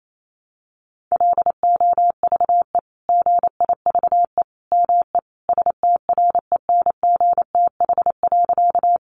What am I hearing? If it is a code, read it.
Morse code, heard as a keyed tone, about 28 wpm, 700 Hz. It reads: LOVE GIVE ME STRENGTH.
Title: Sentences from Top 1300 Words 28wpm